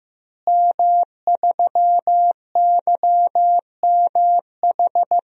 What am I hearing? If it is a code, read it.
Morse code, heard as a keyed tone, about 15 wpm, 700 Hz. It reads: M3YMH
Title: DXCC-100 Call Signs - 5 Characters - Repeated 2x - 15wpm